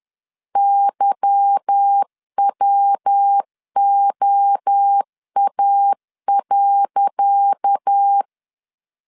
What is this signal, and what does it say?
Morse code, heard as a keyed tone, about 10 wpm, 800 Hz. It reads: YWOA.
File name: LEZIONI / 10 SIGNS / 10C DOT 10wpm 2940